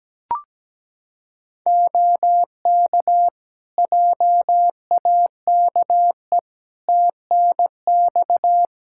Morse code, heard as a keyed tone, about 17 wpm, 700 Hz. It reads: OK JAKE TNX
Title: Practice QSO - Exchange 4 - Ending - 17wpm